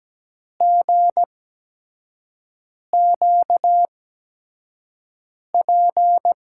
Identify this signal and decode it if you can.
Morse code, heard as a keyed tone, about 17 wpm, spaced as for 5 wpm, 700 Hz. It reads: GQP
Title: USA Call Signs - 6 Characters - 8x Character Spacing - 17wpm